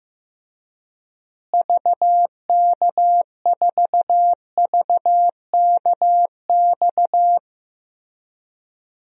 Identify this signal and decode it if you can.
Morse code, heard as a keyed tone, about 15 wpm, 700 Hz. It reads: VK4VKX
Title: DXCC-100 Call Signs - 6 Characters - Repeated 2x - 15wpm